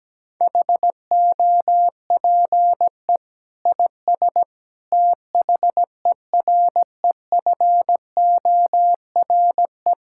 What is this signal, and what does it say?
Morse code, heard as a keyed tone, about 17 wpm, 700 Hz. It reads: HOPE IS THEREFORE
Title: Sentences from Top 1000 Words - Review Part 5 - 17wpm